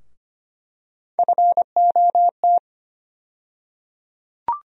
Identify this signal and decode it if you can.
Morse code, heard as a keyed tone, about 25 wpm, 700 Hz. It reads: FOT
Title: USA Call Sign Postfixes - 3 Letters 25wpm